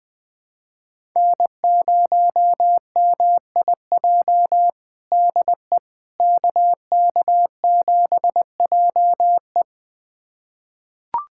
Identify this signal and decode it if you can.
Morse code, heard as a keyed tone, about 20 wpm, 700 Hz. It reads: N0MIJ DE KK7JE
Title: USA Callsign DE Callsign - 5 Characters - 20wpm